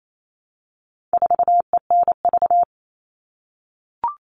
Morse code, heard as a keyed tone, about 28 wpm, 700 Hz. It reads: 4ENV
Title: All Cut Numbers - 4-Digit 28wpm